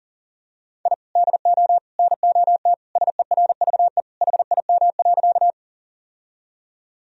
Morse code, heard as a keyed tone, about 40 wpm, 700 Hz. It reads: I DO NOT SERVE HIM.
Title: Sentences from Top 1000 Words - Review Part 2 - 40wpm